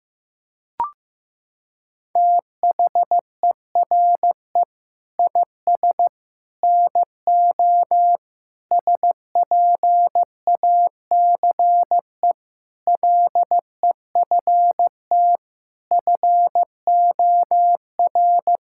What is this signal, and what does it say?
Morse code, heard as a keyed tone, about 15 wpm, 700 Hz. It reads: THERE IS NO SPACE LEFT FOR
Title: Sentences from Top 1000 Words - Review Part 5 - 15wpm